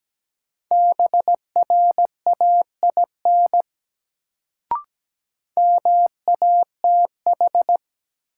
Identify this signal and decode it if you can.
Morse code, heard as a keyed tone, about 17 wpm, 700 Hz. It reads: BRAIN MATH
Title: Sentences from Top 1000 Words - Review Part 6 - New Word Drill - 17wpm